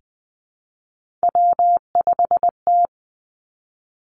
Morse code, heard as a keyed tone, about 20 wpm, 700 Hz. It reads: W5T
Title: Three Challenging Letter-Number Combinations 20wpm